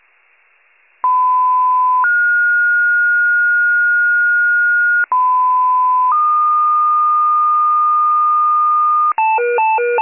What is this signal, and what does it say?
Two-tone page, heard as a one-second tone, then a three-second tone, about 1010 then 1430 Hz, about 1010 then 1220 Hz, with noise. It is heard from 1.0 to 9.1 s.